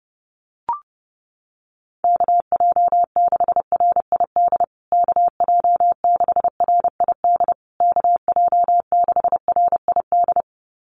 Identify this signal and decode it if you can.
Morse code, heard as a keyed tone, about 30 wpm, 700 Hz. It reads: KJ6RID KJ6RID KJ6RID
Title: USA Call Signs Repeated 3x 30wpm